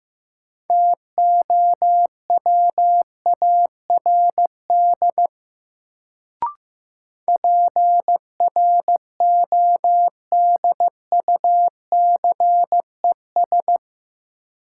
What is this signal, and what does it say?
Morse code, heard as a keyed tone, about 15 wpm, 700 Hz. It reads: TOWARD PRODUCES
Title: Sentences from Top 1000 Words - Review Part 6 - New Word Drill - 15wpm